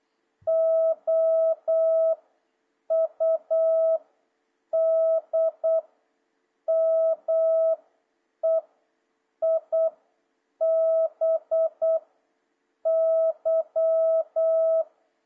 Morse code, heard as a keyed tone, about 8 wpm, 650 Hz. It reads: O U D M E I B Y
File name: jh8cbh / 2026cw